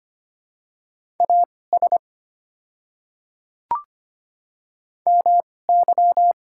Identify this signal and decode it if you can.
Morse code, heard as a keyed tone, about 25 wpm, spaced as for 18 wpm, 700 Hz. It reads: AS MY
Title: 2x Character Spacing - Top 200 Words - 25wpm